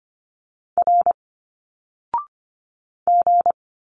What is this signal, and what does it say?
Morse code, heard as a keyed tone, about 25 wpm, 700 Hz. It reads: R G
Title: Single Letters 25wpm